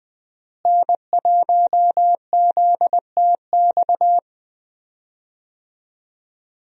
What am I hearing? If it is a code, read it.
Morse code, heard as a keyed tone, about 20 wpm, 700 Hz. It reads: N1ZTX
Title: USA Call Signs - 5 Characters - Repeated 3x - 20wpm